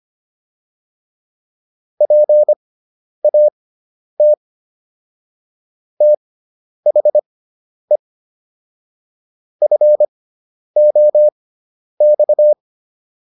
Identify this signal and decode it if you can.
Morse code, heard as a keyed tone, about 25 wpm, spaced as for 10 wpm, 600 Hz. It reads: PAT THE FOX